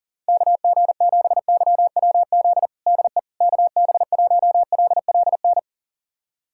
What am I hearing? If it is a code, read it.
Morse code, heard as a keyed tone, about 40 wpm, 700 Hz. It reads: KG7YWZ DE KB1LLN
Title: USA Callsign DE Callsign - 6 Characters - 40wpm